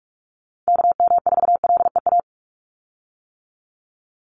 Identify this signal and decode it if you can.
Morse code, heard as a keyed tone, about 44 wpm, 700 Hz. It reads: KM4LEA
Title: USA Call Signs 45wpm